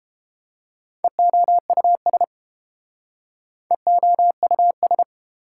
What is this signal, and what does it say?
Morse code, heard as a keyed tone, about 31 wpm, 700 Hz. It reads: EOUS EOUS